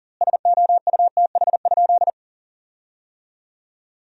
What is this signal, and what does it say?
Morse code, heard as a keyed tone, about 40 wpm, 700 Hz. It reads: SOUTH?
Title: 2x Word-Spacing - Sentences from Top 400 Words - 40wpm